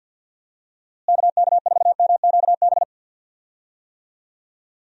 Morse code, heard as a keyed tone, about 51 wpm, 700 Hz. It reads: KK4MQB
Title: USA Call Signs - 6 Characters - Repeated 2x - 50wpm